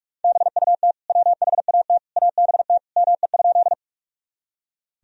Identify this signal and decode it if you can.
Morse code, heard as a keyed tone, about 41 wpm, 700 Hz. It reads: BUT WHAT ABT ME?
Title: Sentences from Top 200 Words and Abbreviations 45wpm